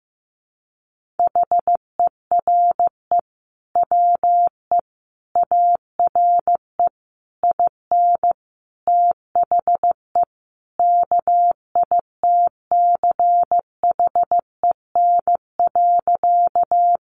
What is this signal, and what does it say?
Morse code, heard as a keyed tone, about 15 wpm, 700 Hz. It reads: HERE WE ARE IN THE KITCHEN.
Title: Sentences from Top 1400 Words 15wpm